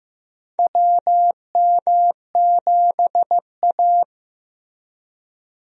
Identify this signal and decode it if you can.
Morse code, heard as a keyed tone, about 15 wpm, 700 Hz. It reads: WM7A
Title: USA Call Signs - 4 Characters - Repeated 3x - 15wpm